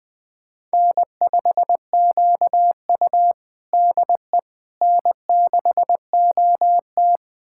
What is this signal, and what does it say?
Morse code, heard as a keyed tone, about 20 wpm, 700 Hz. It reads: N5QU DE N6OT
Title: USA Callsign DE Callsign - 4 Characters - 20wpm